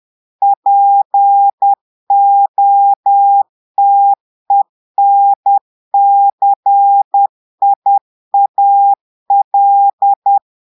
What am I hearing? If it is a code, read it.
Morse code, heard as a keyed tone, about 10 wpm, 800 Hz. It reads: POTENCIAL
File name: Portugais - texte en clair 1 - vitesse - 10 mots minute